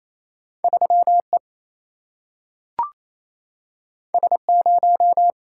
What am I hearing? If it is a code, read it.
Morse code, heard as a keyed tone, about 28 wpm, 700 Hz. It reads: 3 E S 0
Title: Single-Letter Single-Number 28wpm